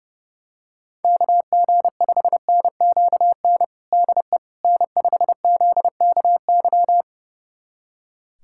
Morse code, heard as a keyed tone, about 30 wpm, 690 Hz. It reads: KG5NQN DE N5ZKY
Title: USA Callsign DE Callsign 30wpm